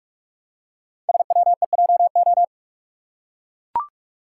Morse code, heard as a keyed tone, about 44 wpm, 700 Hz. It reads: SWEJO
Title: Code Groups - 5 Letters - 45wpm